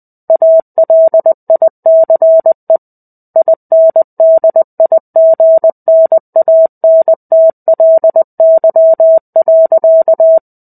Morse code, heard as a keyed tone, about 20 wpm, 650 Hz. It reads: ALICE INDIGNANTLY.
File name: 05-Chapter-05 20 WPM